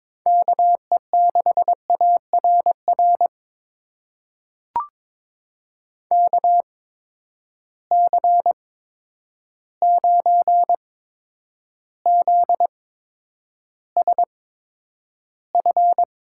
Morse code, heard as a keyed tone, about 22 wpm, spaced as for 9 wpm, 700 Hz. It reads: KE6ARR KC9ZSF
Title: USA Call Signs - 6 Characters - 8x Character Spacing - 22wpm